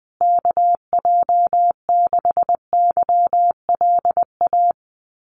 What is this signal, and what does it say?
Morse code, heard as a keyed tone, about 20 wpm, 700 Hz. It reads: KJ6YLA
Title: USA Call Signs Repeated 2x 20wpm